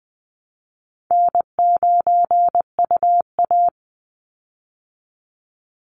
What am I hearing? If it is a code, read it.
Morse code, heard as a keyed tone, about 20 wpm, 700 Hz. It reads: N9UA
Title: Easy Call Signs Repeated 2x - 20wpm